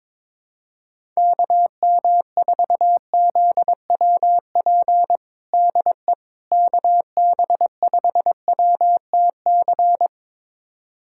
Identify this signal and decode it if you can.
Morse code, heard as a keyed tone, about 22 wpm, 700 Hz. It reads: KM4ZWP DE KB5WTC
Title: USA Callsign DE Callsign 22wpm